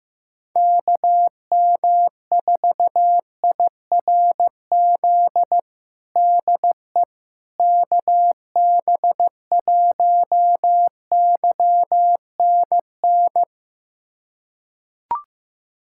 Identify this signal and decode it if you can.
Morse code, heard as a keyed tone, about 15 wpm, 700 Hz. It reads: KM4IRZ DE KB1YNN
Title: USA Callsign DE Callsign - 6 Characters - 15wpm